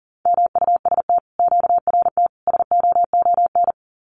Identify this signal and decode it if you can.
Morse code, heard as a keyed tone, about 40 wpm, 700 Hz. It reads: MUST QRT SOON
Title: Common QSO Phrases Encore 40wpm